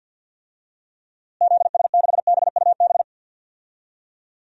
Morse code, heard as a keyed tone, about 50 wpm, 700 Hz. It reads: ZS6BUB
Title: DXCC-100 Call Signs - 6 Characters - Repeated 2x - 50wpm